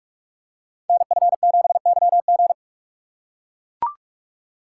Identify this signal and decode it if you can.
Morse code, heard as a keyed tone, about 44 wpm, 700 Hz. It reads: NF7YG
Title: Code Groups - 5 Letter:Numbers - 45wpm